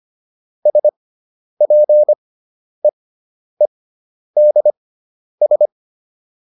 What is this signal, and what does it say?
Morse code, heard as a keyed tone, about 25 wpm, spaced as for 10 wpm, 600 Hz. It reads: SPEEDS